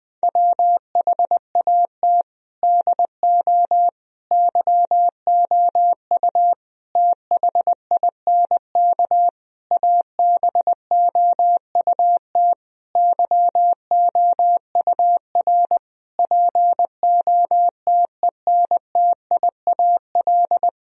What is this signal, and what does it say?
Morse code, heard as a keyed tone, about 20 wpm, 700 Hz. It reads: WHAT DO YOU THINK ABOUT YOUR POTENTIAL
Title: Sentences from Top 1000 Words - Review Part 4 - 20wpm